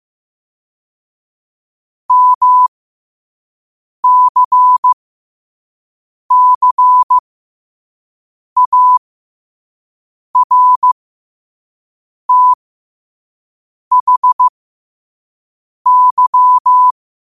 Morse code, heard as a keyed tone, about 15 wpm, spaced as for 5 wpm, 1000 Hz. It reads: MCCARTHY
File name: qotdfiles / 5wpm